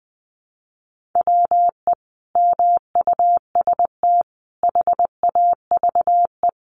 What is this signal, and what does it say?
Morse code, heard as a keyed tone, about 20 wpm, 700 Hz. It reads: WE MUST HAVE
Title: Sentences from Top 1300 Words 20wpm